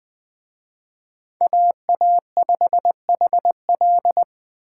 Morse code, heard as a keyed tone, about 20 wpm, 700 Hz. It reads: AA5HL